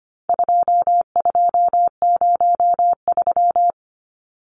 Morse code, heard as a keyed tone, about 25 wpm, 700 Hz. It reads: 2203
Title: Numbers - 4-Digit 25wpm